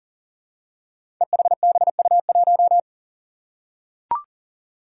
Morse code, heard as a keyed tone, about 38 wpm, 700 Hz. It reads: EHBU1
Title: Five Challenging Letter-Number Combinations 40wpm